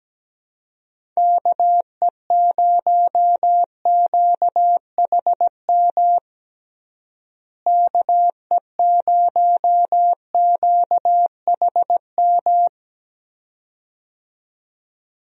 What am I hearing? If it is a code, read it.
Morse code, heard as a keyed tone, about 17 wpm, 700 Hz. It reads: KE0QHM KE0QHM
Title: USA Call Signs - 6 Characters - Repeated 2x - 17wpm